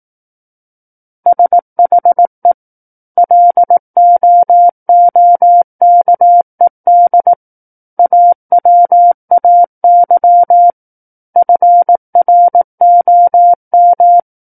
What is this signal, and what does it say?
Morse code, heard as a keyed tone, about 18 wpm, 700 Hz. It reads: SHE LOOKED AWAY FROM
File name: 18 WPM